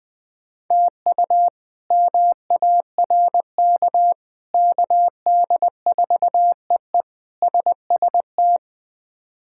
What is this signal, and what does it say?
Morse code, heard as a keyed tone, about 20 wpm, 700 Hz. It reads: TU MARK KD4EE SST
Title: K1USN SST QSO Practice - 20wpm